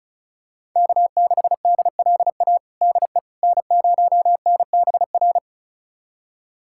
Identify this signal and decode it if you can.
Morse code, heard as a keyed tone, about 36 wpm, 700 Hz. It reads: K6DLA DE N0NBR
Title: USA Callsign DE Callsign - 5 Characters - 35wpm